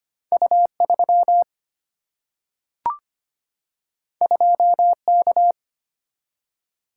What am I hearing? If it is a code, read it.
Morse code, heard as a keyed tone, about 25 wpm, 700 Hz. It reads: U3 2K
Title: Single-Letter Single-Number 25wpm